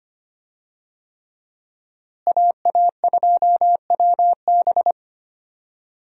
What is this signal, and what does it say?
Morse code, heard as a keyed tone, about 25 wpm, 700 Hz. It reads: AA2WB
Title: USA Call Signs - 5 Characters - Repeated 3x - 25wpm